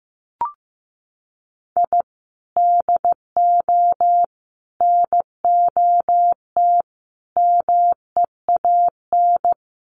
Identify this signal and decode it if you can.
Morse code, heard as a keyed tone, about 15 wpm, 700 Hz. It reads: I DO NOT MEAN
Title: Sets of 5 Words 15wpm